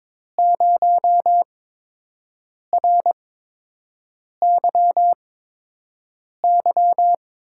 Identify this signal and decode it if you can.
Morse code, heard as a keyed tone, about 22 wpm, spaced as for 6 wpm, 700 Hz. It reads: 0RYY
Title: USA Call Signs - 5 Characters - 8x Character Spacing - 22wpm